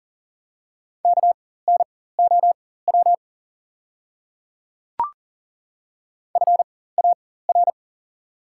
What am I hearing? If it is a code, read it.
Morse code, heard as a keyed tone, about 40 wpm, spaced as for 19 wpm, 700 Hz. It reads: KNOW FAR
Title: 4x Character Spacing - Top 300 Words - 40wpm